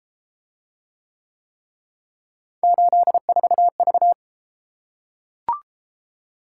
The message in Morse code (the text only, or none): 84V